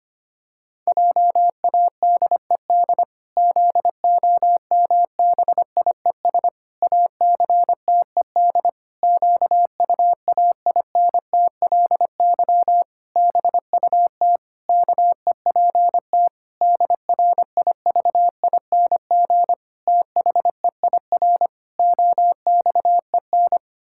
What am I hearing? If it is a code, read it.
Morse code, heard as a keyed tone, about 25 wpm, 700 Hz. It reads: JADED ZOMBIES ACTED QUAINTLY BUT KEPT DRIVING THEIR OXEN
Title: Pangram Challenge v2 - 25wpm